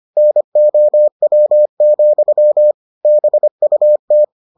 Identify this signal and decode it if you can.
Morse code, heard as a keyed tone, about 25 wpm, 600 Hz. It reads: NOW, BUT